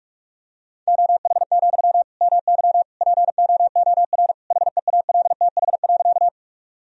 Morse code, heard as a keyed tone, about 41 wpm, 700 Hz. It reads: OH, MY POOR HEALTH.
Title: Sentences from Top 1000 Words - Review Part 3 - 45wpm